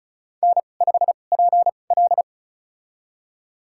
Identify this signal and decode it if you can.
Morse code, heard as a keyed tone, about 36 wpm, spaced as for 25 wpm, 700 Hz. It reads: N5PL
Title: USA Call Signs - 4 Characters - 2x Character Spacing - 35wpm